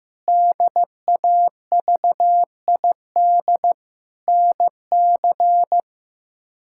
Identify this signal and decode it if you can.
Morse code, heard as a keyed tone, about 15 wpm, 700 Hz. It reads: DAVID NC
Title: K1USN SST QSO Practice - 15wpm